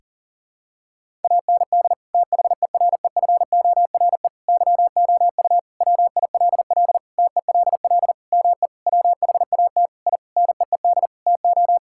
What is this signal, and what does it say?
Morse code, heard as a keyed tone, about 40 wpm, 700 Hz. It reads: AND THEREFORE YOU WILL TELL ME WHAT I NEED TO
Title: Sentences from Top 1000 Words - Review Part 5 - 40wpm